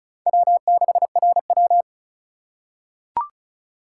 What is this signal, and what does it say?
Morse code, heard as a keyed tone, about 36 wpm, 700 Hz. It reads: W6RW
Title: USA Call Signs - 4 Characters - 3x Character Spacing - 35wpm